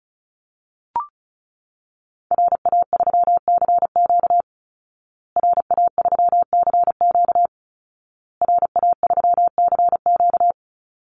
Morse code, heard as a keyed tone, about 34 wpm, 700 Hz. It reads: RA3CQ RA3CQ RA3CQ
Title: Call Signs Repeated 2x - 35wpm